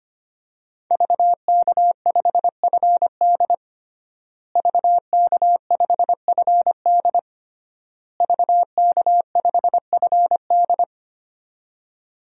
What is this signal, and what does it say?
Morse code, heard as a keyed tone, about 25 wpm, 700 Hz. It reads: VK5FD VK5FD VK5FD
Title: DXCC-100 Call Signs - 5 Characters - Repeated 3x - 25wpm